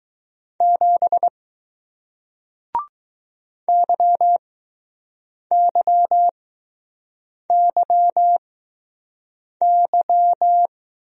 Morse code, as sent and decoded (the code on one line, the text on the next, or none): --... / -.-- -.-- -.-- -.--
7 YYYY